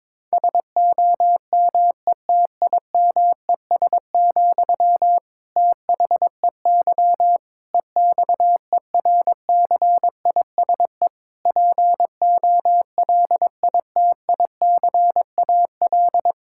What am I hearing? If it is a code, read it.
Morse code, heard as a keyed tone, about 22 wpm, 700 Hz. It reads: SOMETIMES, THEY EXERCISE POLITICAL
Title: Sentences from Top 1000 Words - Review Part 6 - 22wpm